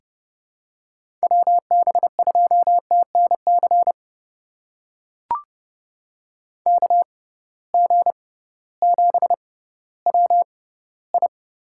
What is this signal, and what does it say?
Morse code, heard as a keyed tone, about 31 wpm, spaced as for 16 wpm, 700 Hz. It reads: WB2TNC KG7WI
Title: USA Call Signs - 6 Characters - 6x Character Spacing - 30wpm